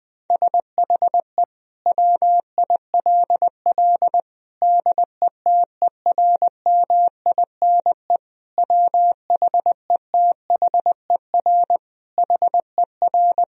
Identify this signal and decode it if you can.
Morse code, heard as a keyed tone, about 20 wpm, 700 Hz. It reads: SHE WILL DETERMINE WHETHER HER
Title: Sentences from Top 1000 Words - Review Part 4 - 20wpm